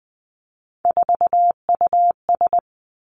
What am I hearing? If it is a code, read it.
Morse code, heard as a keyed tone, about 20 wpm, 700 Hz. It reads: 4US
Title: Three Challenging Letter-Number Combinations 20wpm